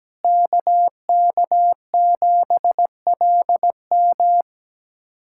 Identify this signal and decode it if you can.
Morse code, heard as a keyed tone, about 17 wpm, 700 Hz. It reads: KK7LM
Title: USA Callsign DE Callsign - 5 Characters - 17wpm